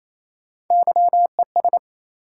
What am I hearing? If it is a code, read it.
Morse code, heard as a keyed tone, about 28 wpm, 700 Hz. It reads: YES
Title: Sentences from Top 500 Words - New Word Drill - 28wpm